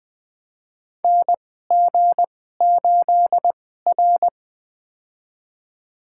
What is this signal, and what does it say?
Morse code, heard as a keyed tone, about 20 wpm, spaced as for 14 wpm, 700 Hz. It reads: NG8R
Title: USA Call Signs - 4 Characters - 2x Character Spacing - 20wpm